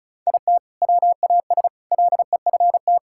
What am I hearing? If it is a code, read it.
Morse code, heard as a keyed tone, about 34 wpm, 700 Hz. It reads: IT WAS LEFT
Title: Sentences from Top 1000 Words - Review Part 6 - 35wpm